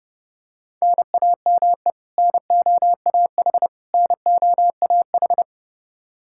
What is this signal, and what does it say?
Morse code, heard as a keyed tone, about 30 wpm, 700 Hz. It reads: NAME NOAH NOAH
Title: Practice QSO - Exchange 1 - Beginning - 30wpm